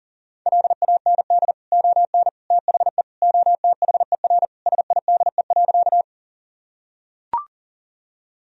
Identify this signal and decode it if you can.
Morse code, heard as a keyed tone, about 40 wpm, 700 Hz. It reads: LAND ON THE OTHER SIDE.
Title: Sentences from Top 1000 Words - Review Part 1 - 40wpm